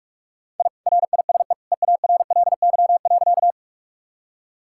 Morse code, heard as a keyed tone, about 47 wpm, 700 Hz. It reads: I RISE EARLY.
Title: Sentences from Top 1000 Words - Review Part 6 - 45wpm